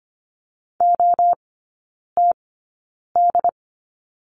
Morse code, heard as a keyed tone, about 25 wpm, spaced as for 9 wpm, 700 Hz. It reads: OTD
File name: course / 25:10wpm / 093 Lesson 16e - Letter c - Three Characters 25wpm